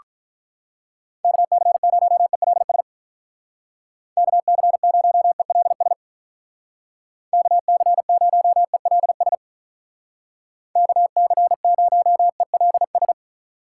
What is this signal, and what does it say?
Morse code, heard as a keyed tone, about 44 wpm, 700 Hz. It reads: KC0ELS KC0ELS KC0ELS KC0ELS